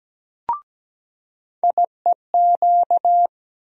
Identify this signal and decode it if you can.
Morse code, heard as a keyed tone, about 17 wpm, 700 Hz. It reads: IEQ
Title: USA Call Sign Postfixes - 3 Letters 17wpm